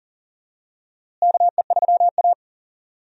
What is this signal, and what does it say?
Morse code, heard as a keyed tone, about 40 wpm, 700 Hz. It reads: KE3A
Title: USA Call Signs - 4 Characters - Repeated 2x - 40wpm